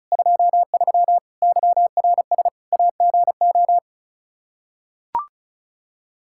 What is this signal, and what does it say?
Morse code, heard as a keyed tone, about 36 wpm, 700 Hz. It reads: 23 YRS AGO
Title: K1USN SST QSO Practice 35wpm